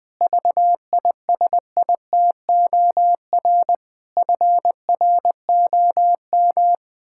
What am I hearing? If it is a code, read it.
Morse code, heard as a keyed tone, about 20 wpm, 700 Hz. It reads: VISITOR FROM
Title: Pangram Challenge v2 - 20wpm